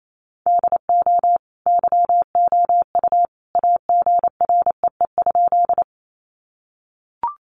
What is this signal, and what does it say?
Morse code, heard as a keyed tone, about 28 wpm, 700 Hz. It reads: DO YOU AGREE?
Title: Sentences from Top 1100 Words 28wpm